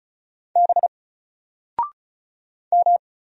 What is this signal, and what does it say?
Morse code, heard as a keyed tone, about 36 wpm, 700 Hz. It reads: B M
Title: Single Letter-Number Alt 35wpm